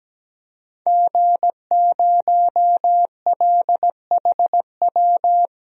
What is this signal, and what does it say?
Morse code, heard as a keyed tone, about 17 wpm, 700 Hz. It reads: G0LHW
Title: DXCC-100 Call Signs - 5 Characters - 17wpm